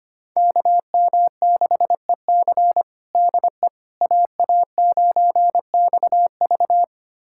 Morse code, heard as a keyed tone, about 25 wpm, 700 Hz. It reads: KM6EC DE AA9XV
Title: USA Callsign DE Callsign - 5 Characters - 25wpm